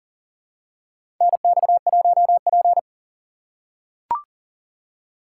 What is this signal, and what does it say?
Morse code, heard as a keyed tone, about 40 wpm, 700 Hz. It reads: NX1P
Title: USA Call Signs - 4 Characters - 4x Character Spacing - 40wpm